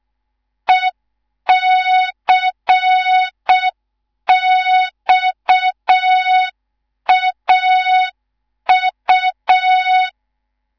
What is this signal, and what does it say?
Morse code, heard as a keyed tone, about 6 wpm, 3700 Hz. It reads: ECXAU
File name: Morze-HA2MN / hang / 19 GYAKORLAT